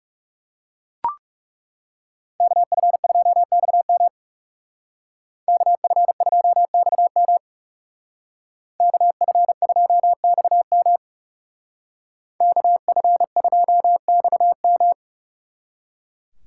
Morse code, heard as a keyed tone, about 37 wpm, 700 Hz. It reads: KF2XM KF2XM KF2XM KF2XM
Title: Speed Racing - USA 2x2 Call Signs 30wpm